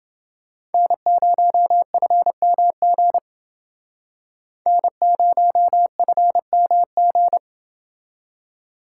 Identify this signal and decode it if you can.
Morse code, heard as a keyed tone, about 29 wpm, 700 Hz. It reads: N0FMG N0FMG